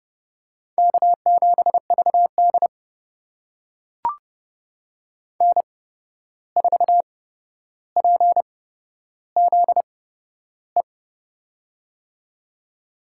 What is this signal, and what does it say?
Morse code, heard as a keyed tone, about 30 wpm, spaced as for 12 wpm, 700 Hz. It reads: K7VD N4PZE
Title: USA Call Signs - 8x Character Spacing - 30wpm